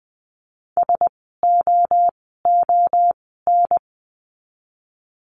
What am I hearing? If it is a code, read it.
Morse code, heard as a keyed tone, about 20 wpm, spaced as for 14 wpm, 700 Hz. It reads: SOON
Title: 054 Lesson 10f - Letter r - Words 20wpm